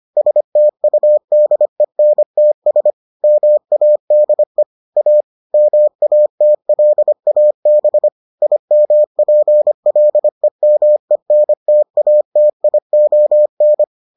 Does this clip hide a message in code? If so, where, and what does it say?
Yes: Morse code from 0.2 to 13.8 s, heard as a keyed tone, about 25 wpm, 600 Hz. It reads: STUDENTS MADE A MATLAB IMPLEMENTATION